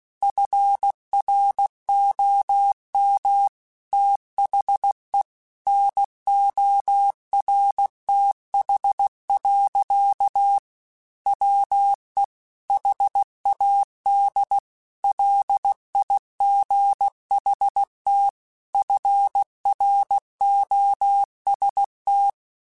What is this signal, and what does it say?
Morse code, heard as a keyed tone, about 16 wpm, 800 Hz. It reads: FROM THE NORTH. WE HAD LIGHT FROST
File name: Weather Sets / Weather Sets-mp3s / Weather 4 24 @15